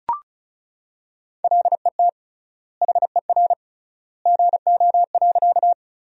Morse code, heard as a keyed tone, about 36 wpm, 700 Hz. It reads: LET HER GO.
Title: 3x Word-Spacing - Sentences from Top 300 Words - 35wpm